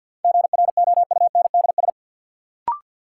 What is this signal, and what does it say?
Morse code, heard as a keyed tone, about 49 wpm, 700 Hz. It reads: GROUNDS
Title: Sentences from Top 1300 Words - New Word Drill Alt - 50wpm